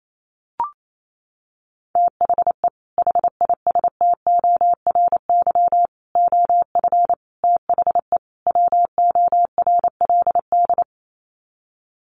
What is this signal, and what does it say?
Morse code, heard as a keyed tone, about 28 wpm, 700 Hz. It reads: THE HISTORY OF THE WORLD
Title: Sets of 5 Words Encore2 28wpm